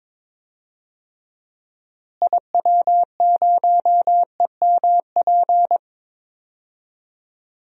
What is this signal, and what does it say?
Morse code, heard as a keyed tone, about 22 wpm, 700 Hz. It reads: IW0EMP
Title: DXCC-100 Call Signs - 6 Characters - Repeated 3x - 22wpm